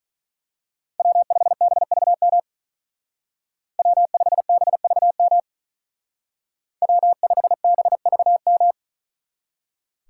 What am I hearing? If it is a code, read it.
Morse code, heard as a keyed tone, about 41 wpm, 700 Hz. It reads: W5BVM W5BVM W5BVM
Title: Speed Racing - USA 1x3 Call Signs 35wpm